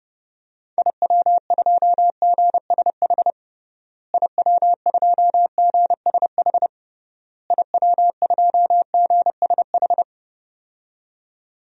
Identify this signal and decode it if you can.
Morse code, heard as a keyed tone, about 30 wpm, 700 Hz. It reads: IW2GSH IW2GSH IW2GSH
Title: DXCC-100 Call Signs - 6 Characters - Repeated 3x - 30wpm